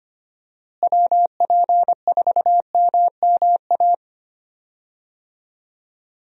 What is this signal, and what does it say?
Morse code, heard as a keyed tone, about 25 wpm, 700 Hz. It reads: WP4MMA
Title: USA Call Signs - 6 Characters - Repeated 2x - 25wpm